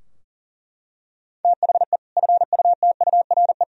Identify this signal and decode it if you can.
Morse code, heard as a keyed tone, about 37 wpm, 700 Hz. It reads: THE FUTURE
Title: Sentences from Top 1000 Words - Review Part 3 - 40wpm